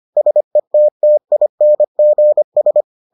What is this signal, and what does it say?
Morse code, heard as a keyed tone, about 25 wpm, 600 Hz. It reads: SETTINGS